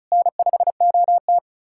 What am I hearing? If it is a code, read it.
Morse code, heard as a keyed tone, about 47 wpm, spaced as for 41 wpm, 700 Hz. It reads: N5OT
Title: Speed-Racing Common QSO Phrases - 35wpm